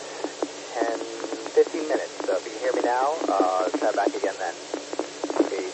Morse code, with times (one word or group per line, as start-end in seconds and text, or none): none